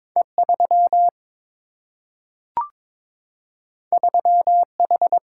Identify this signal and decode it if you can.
Morse code, heard as a keyed tone, about 22 wpm, 700 Hz. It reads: E 3 3 H
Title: Single-Letter Single-Number 22wpm